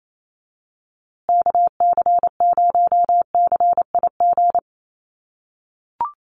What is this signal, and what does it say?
Morse code, heard as a keyed tone, about 27 wpm, 700 Hz. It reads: KC0CIG
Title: USA Call Signs Repeated 2x 28wpm